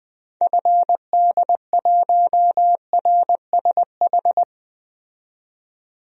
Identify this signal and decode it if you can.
Morse code, heard as a keyed tone, about 20 wpm, 700 Hz. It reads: FD1RSH
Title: DXCC-100 Call Signs - 6 Characters - 20wpm